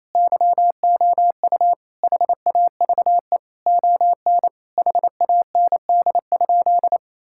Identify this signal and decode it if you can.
Morse code, heard as a keyed tone, about 28 wpm, 700 Hz. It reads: YOU HAVE ON HAND?
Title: Sentences from Top 1000 Words - Review Part 5 - 28wpm